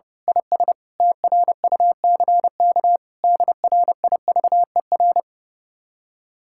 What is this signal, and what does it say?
Morse code, heard as a keyed tone, about 28 wpm, 700 Hz. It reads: IS TRUCK DRIVER
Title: Speed-Racing Common QSO Phrases - 30wpm